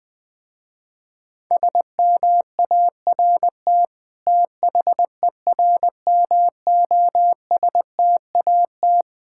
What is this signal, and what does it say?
Morse code, heard as a keyed tone, about 20 wpm, 700 Hz. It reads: SMART THERMOSTAT
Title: Difficult Words with Hints - 20wpm